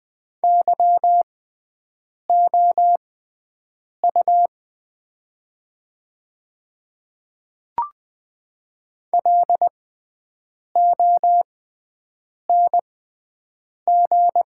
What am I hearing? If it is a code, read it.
Morse code, heard as a keyed tone, about 20 wpm, spaced as for 7 wpm, 700 Hz. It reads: YOU LONG
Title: 6x Character Spacing - Top 200 Words - 20wpm